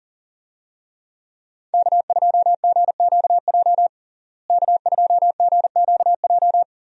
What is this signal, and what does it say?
Morse code, heard as a keyed tone, about 40 wpm, 700 Hz. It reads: K2GQJ K2GQJ
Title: USA Call Signs - 5 Characters - Repeated 2x - 40wpm